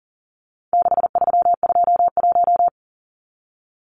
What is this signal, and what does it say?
Morse code, heard as a keyed tone, about 40 wpm, 700 Hz. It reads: 6321
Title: Numbers - 4-Digit 40wpm